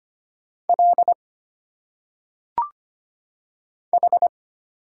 Morse code, heard as a keyed tone, about 25 wpm, 700 Hz. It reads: L H
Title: Single Letters - Phonetic Pronunciation 25wpm